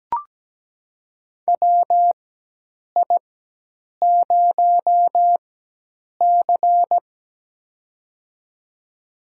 Morse code, heard as a keyed tone, about 17 wpm, spaced as for 8 wpm, 700 Hz. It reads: WI0C
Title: USA Call Signs - 4 Characters - 4x Character Spacing - 17wpm